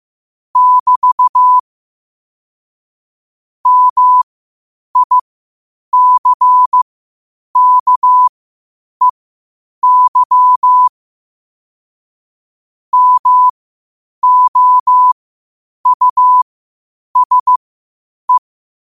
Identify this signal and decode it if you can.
Morse code, heard as a keyed tone, about 15 wpm, spaced as for 8 wpm, 1000 Hz. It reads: <BT> MICKEY MOUSE
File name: qotdfiles / 8wpm